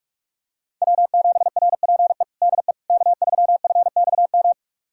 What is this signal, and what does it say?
Morse code, heard as a keyed tone, about 42 wpm, 700 Hz. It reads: W7RPE DE K3FXM
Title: USA Callsign DE Callsign - 5 Characters - 45wpm